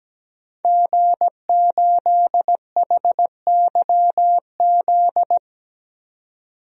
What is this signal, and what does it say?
Morse code, heard as a keyed tone, about 17 wpm, 700 Hz. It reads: G8HYZ
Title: DXCC-100 Call Signs - 5 Characters - 17wpm